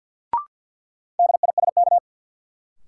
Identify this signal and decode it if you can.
Morse code, heard as a keyed tone, about 53 wpm, 700 Hz. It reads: DISK
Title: Sentences from Top 1000 Words - Review Part 6 - New Word Drill - 50wpm